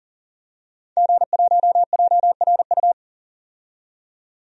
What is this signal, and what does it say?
Morse code, heard as a keyed tone, about 38 wpm, 700 Hz. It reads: G1JRU
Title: DXCC-100 Call Signs - 5 Characters - 40wpm